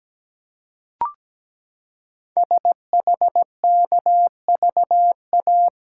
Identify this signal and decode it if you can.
Morse code, heard as a keyed tone, about 17 wpm, 700 Hz. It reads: SHKVA
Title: Code Groups - 5 Letters - 17wpm